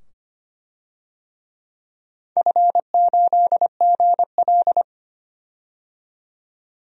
Morse code, heard as a keyed tone, about 25 wpm, 700 Hz. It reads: F8GL
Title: DXCC-100 Call Signs - 4 Characters - 25wpm